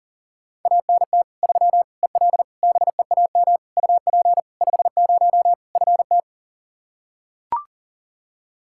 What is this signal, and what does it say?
Morse code, heard as a keyed tone, about 40 wpm, 700 Hz. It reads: ANT 3 EL BEAM UP 50 FT
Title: Practice QSO - Exchange 2 - Info - 40wpm